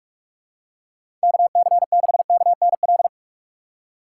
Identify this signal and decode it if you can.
Morse code, heard as a keyed tone, about 45 wpm, 700 Hz. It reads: KC6KNL
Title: USA Call Signs - 6 Characters - 3x Character Spacing - 45wpm